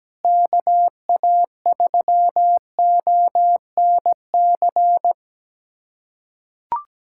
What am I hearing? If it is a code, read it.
Morse code, heard as a keyed tone, about 17 wpm, 700 Hz. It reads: KA3ONC
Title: USA Callsign DE Callsign 17wpm